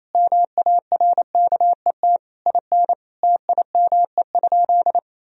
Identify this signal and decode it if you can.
Morse code, heard as a keyed tone, about 28 wpm, 700 Hz. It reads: MARKET IN TIME?
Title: Sentences from Top 1000 Words - Review Part 3 - 28wpm